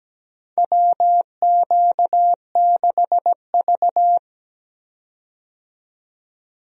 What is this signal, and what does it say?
Morse code, heard as a keyed tone, about 17 wpm, 700 Hz. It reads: WQ6V
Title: USA Call Signs - 4 Characters - 17wpm